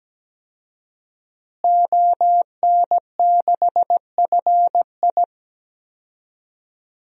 Morse code, heard as a keyed tone, about 17 wpm, 700 Hz. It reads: ON6FI
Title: DXCC-100 Call Signs - 5 Characters - Repeated 3x - 17wpm